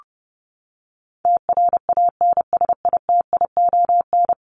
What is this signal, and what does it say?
Morse code, heard as a keyed tone, about 30 wpm, 700 Hz. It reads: TRANSITION
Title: Top 1751-2000 Words 30wpm